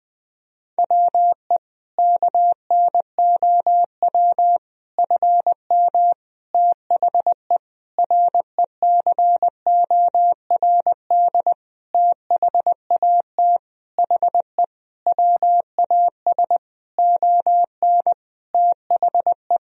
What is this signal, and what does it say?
Morse code, heard as a keyed tone, about 21 wpm, 700 Hz. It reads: WE KNOW FM THE RECORD THAT HE WAS ON THE
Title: Sentences from Top 500 Words and Abbreviations 20wpm